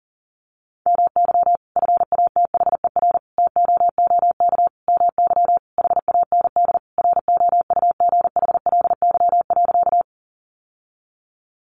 Morse code, heard as a keyed tone, about 37 wpm, 700 Hz. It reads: MY FATHER TOOK MY HAND ROUGHLY.
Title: Sentences from Top 1500 Words 40wpm